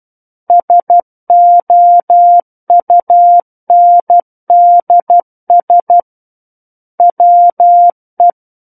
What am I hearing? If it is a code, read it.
Morse code, heard as a keyed tone, about 12 wpm, 700 Hz. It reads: SOUNDS WE
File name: Book One / 03-Chapter 3 - 12 WPM